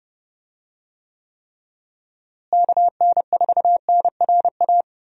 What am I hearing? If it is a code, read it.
Morse code, heard as a keyed tone, about 30 wpm, 700 Hz. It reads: KN4NRA
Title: USA Call Signs - 6 Characters - Repeated 3x - 30wpm